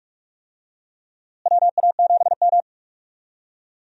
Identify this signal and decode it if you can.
Morse code, heard as a keyed tone, about 43 wpm, 700 Hz. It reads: WA7M
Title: USA Call Signs - 4 Characters - Repeated 2x - 45wpm